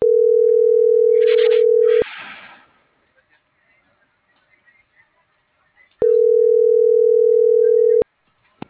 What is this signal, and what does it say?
Ringback tone (North America), 440 + 480 Hz.